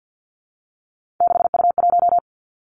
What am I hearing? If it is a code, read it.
Morse code, heard as a keyed tone, about 50 wpm, 700 Hz. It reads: 6U1
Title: Three Challenging Letter-Number Combinations 50wpm